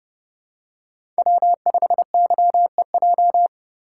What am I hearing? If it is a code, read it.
Morse code, heard as a keyed tone, about 24 wpm, 700 Hz. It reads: W5YEJ
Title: Speed Racing - USA 1x3 Call Signs 20wpm